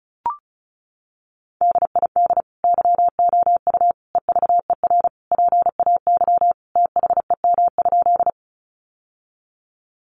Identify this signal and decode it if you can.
Morse code, heard as a keyed tone, about 34 wpm, 700 Hz. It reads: DID YOU EVER PAY THEM?
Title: Sentences from Top 700 Words 35wpm